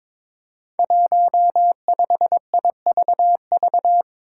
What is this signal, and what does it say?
Morse code, heard as a keyed tone, about 22 wpm, 700 Hz. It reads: 15IVV
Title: Five Challenging Letter-Number Combinations 22wpm